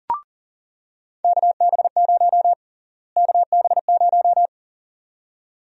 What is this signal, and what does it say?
Morse code, heard as a keyed tone, about 40 wpm, 700 Hz. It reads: KB0 KB0
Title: USA Call Sign Prefixes - Repeated 2x - 40wpm